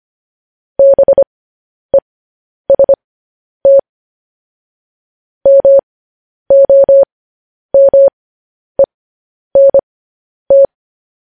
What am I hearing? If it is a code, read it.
Morse code, heard as a keyed tone, about 25 wpm, spaced as for 10 wpm, 550 Hz. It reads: BEST MOMENT
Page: News Headlines in Morse Code at 10 WPM